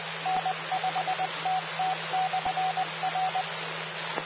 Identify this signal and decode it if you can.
Morse code, heard as a keyed tone, about 20 wpm, 750 Hz. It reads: N5TT/R
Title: Austin, TX Amateur Repeaters